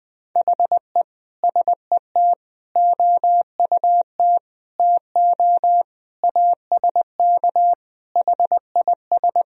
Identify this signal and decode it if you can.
Morse code, heard as a keyed tone, about 21 wpm, 700 Hz. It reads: HE SET OUT TO ASK HIS
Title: Sentences from Top 1000 Words - Review Part 1 - 20wpm